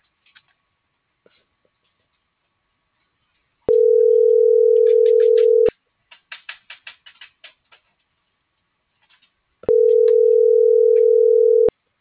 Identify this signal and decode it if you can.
Ringback tone (North America), 440 + 480 Hz.